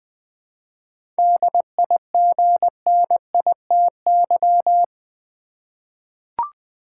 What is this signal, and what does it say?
Morse code, heard as a keyed tone, about 20 wpm, 700 Hz. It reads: DIGNITY
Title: 7 Letter Words - First 3 Letters Given - 20wpm